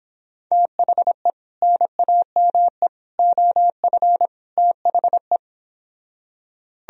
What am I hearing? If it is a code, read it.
Morse code, heard as a keyed tone, about 26 wpm, 700 Hz. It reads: THE NAME OF THE
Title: Speed-Racing - Sets of 4 Words 22wpm